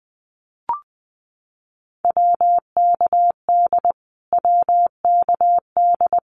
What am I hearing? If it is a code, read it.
Morse code, heard as a keyed tone, about 20 wpm, 700 Hz. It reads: WKD WKD
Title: Abbreviations Repeated 3x - 20wpm